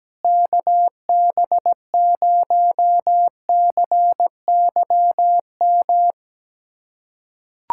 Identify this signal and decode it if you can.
Morse code, heard as a keyed tone, about 17 wpm, 700 Hz. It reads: KB0CYM
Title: USA Callsign DE Callsign 17wpm